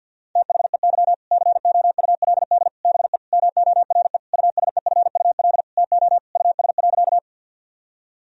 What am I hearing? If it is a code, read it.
Morse code, heard as a keyed tone, about 50 wpm, 700 Hz. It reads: THEY COULD BE MORE USEFUL TO US.
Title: Sentences from Top 1000 Words - Review Part 4 - 50wpm